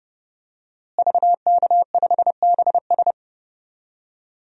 Morse code, heard as a keyed tone, about 31 wpm, 700 Hz. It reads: VK5BS
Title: DXCC-100 Call Signs - 5 Characters - Repeated 2x - 30wpm